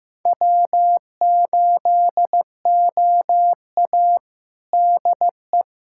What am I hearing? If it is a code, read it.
Morse code, heard as a keyed tone, about 15 wpm, 700 Hz. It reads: W8OA DE
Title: USA Callsign DE Callsign - 4 Characters - 15wpm